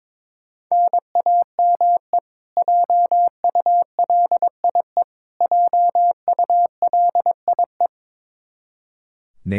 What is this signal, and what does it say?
Morse code, heard as a keyed tone, about 22 wpm, 700 Hz. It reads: NAME JULIE JULIE
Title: Practice QSO - Exchange 1 - Beginning - 22wpm